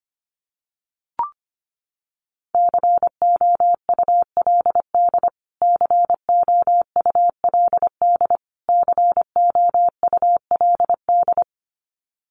Morse code, heard as a keyed tone, about 25 wpm, 700 Hz. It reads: COULD COULD COULD
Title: Top 500 Words Repeated 25wpm